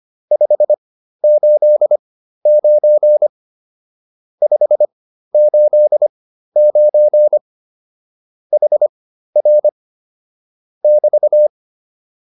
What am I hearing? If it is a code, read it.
Morse code, heard as a keyed tone, about 25 wpm, spaced as for 13 wpm, 600 Hz. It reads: 589 589 HR <BT>
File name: QSO 207 13